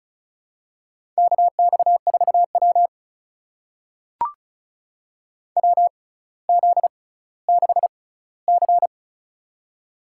Morse code, heard as a keyed tone, about 36 wpm, spaced as for 18 wpm, 700 Hz. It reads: KX4W WZ6C
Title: USA Call Signs - 4 Characters - 6x Character Spacing - 35wpm